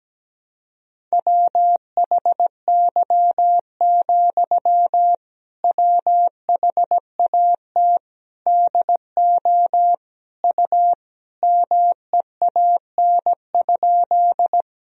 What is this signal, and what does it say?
Morse code, heard as a keyed tone, about 17 wpm, 700 Hz. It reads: WHY, WHAT DO U MEAN?
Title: Sentences from Top 200 Words and Abbreviations 17wpm